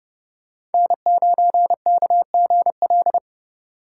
Morse code, heard as a keyed tone, about 31 wpm, 700 Hz. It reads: N9KGL
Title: USA Call Signs - 3x Character Spacing - 30wpm